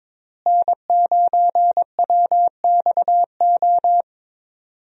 Morse code, heard as a keyed tone, about 21 wpm, 700 Hz. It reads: N9WXO
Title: USA Call Signs - 5 Characters - Repeated 3x - 22wpm